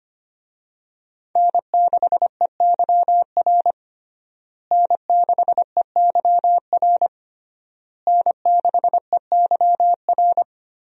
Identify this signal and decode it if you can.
Morse code, heard as a keyed tone, about 25 wpm, 700 Hz. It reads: N6EYR N6EYR N6EYR
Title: USA Call Signs - 5 Characters - Repeated 3x - 25wpm